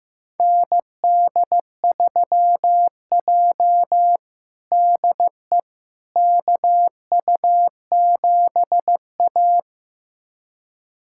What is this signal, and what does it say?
Morse code, heard as a keyed tone, about 16 wpm, 700 Hz. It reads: ND3J DE KU7A
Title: USA Callsign DE Callsign - 4 Characters - 15wpm